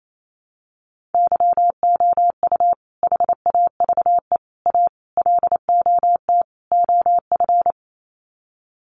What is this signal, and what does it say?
Morse code, heard as a keyed tone, about 28 wpm, 690 Hz. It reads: YOU HAVE A LOT OF
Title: Sets of 5 Words Encore 28wpm